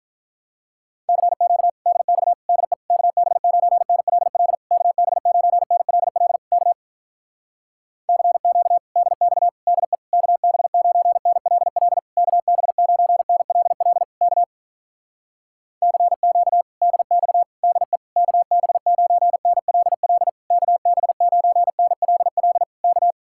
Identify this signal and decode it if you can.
Morse code, heard as a keyed tone, about 46 wpm, 700 Hz. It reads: CQ DX DE KB9NLL KB9NLL K CQ DX DE KB9NLL KB9NLL K CQ DX DE KB9NLL KB9NLL K